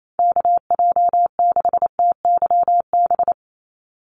Morse code, heard as a keyed tone, about 28 wpm, 700 Hz. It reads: KJ6TYB
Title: USA Call Signs 28wpm